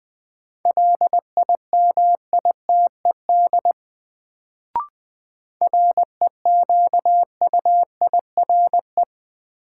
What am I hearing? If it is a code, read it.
Morse code, heard as a keyed tone, about 20 wpm, 700 Hz. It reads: LIMITED REQUIRE
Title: Sentences from Top 900 Words - New Word Drill - 20wpm